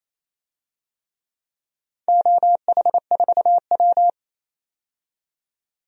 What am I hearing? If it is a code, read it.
Morse code, heard as a keyed tone, about 27 wpm, 700 Hz. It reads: OH4W
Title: DXCC-100 Call Signs - 4 Characters - 28wpm